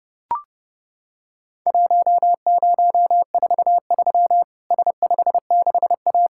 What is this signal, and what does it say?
Morse code, heard as a keyed tone, about 32 wpm, 700 Hz. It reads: 1043 S56A
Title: Worked All Europe DX Contest - QTC Exchange - 30wpm